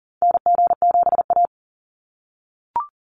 Morse code, heard as a keyed tone, about 40 wpm, 700 Hz. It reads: NG7A
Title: NA SOTA Activator Callsigns 40wpm